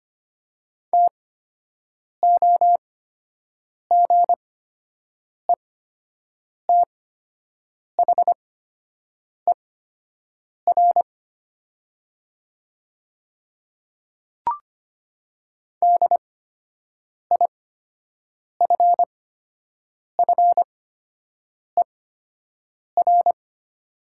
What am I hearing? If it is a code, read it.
Morse code, heard as a keyed tone, about 25 wpm, spaced as for 7 wpm, 700 Hz. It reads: TOGETHER DIFFER
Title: 8x Character Spacing - Top 300 Words - 25wpm